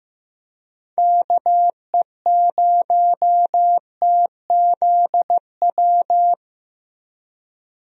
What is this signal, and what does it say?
Morse code, heard as a keyed tone, about 15 wpm, 700 Hz. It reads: KE0TZW
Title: USA Call Signs - 6 Characters - Repeated 3x - 15wpm